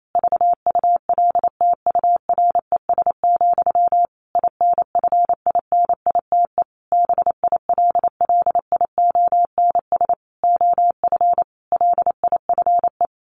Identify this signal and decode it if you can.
Morse code, heard as a keyed tone, about 28 wpm, 700 Hz. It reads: VULTURES, INFINITE BILLIONS OF LIFE
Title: A Sound Of Thunder 28wpm